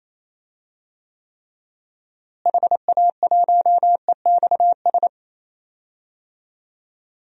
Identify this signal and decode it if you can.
Morse code, heard as a keyed tone, about 27 wpm, 700 Hz. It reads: HA1EXS
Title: DXCC-100 Call Signs - 6 Characters - Repeated 3x - 28wpm